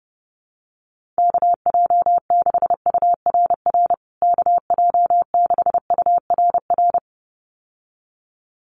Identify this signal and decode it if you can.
Morse code, heard as a keyed tone, about 30 wpm, 700 Hz. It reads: KJ6URR KJ6URR
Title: USA Call Signs Repeated 2x 30wpm